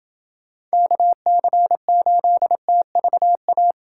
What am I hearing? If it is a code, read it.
Morse code, heard as a keyed tone, about 28 wpm, 700 Hz. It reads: KC8TVA